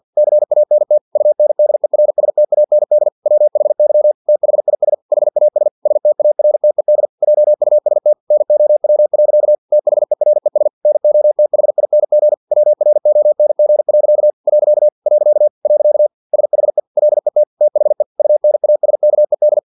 Morse code, heard as a keyed tone, about 50 wpm, 600 Hz. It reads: CANT UNDERSTAND WHY THIS HAS STARTED JUST NOW. THERES NOTHING WRONG. . SHE LET THE UNASKED
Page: Never 49wpm